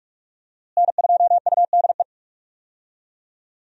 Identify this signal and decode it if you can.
Morse code, heard as a keyed tone, about 41 wpm, 700 Hz. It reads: N2UDE
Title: USA Call Signs - 5 Characters - Repeated 3x - 45wpm